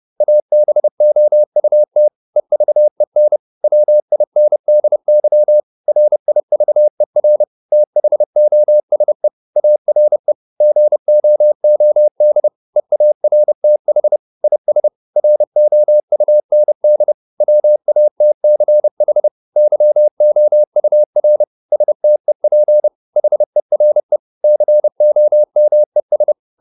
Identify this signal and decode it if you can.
Morse code, heard as a keyed tone, about 30 wpm, 600 Hz. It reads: ABOUT EVEN WINDY RIVER THOSE ARE GOOD EARTH IS ROUND WATCH YOUR STEP HERE COMES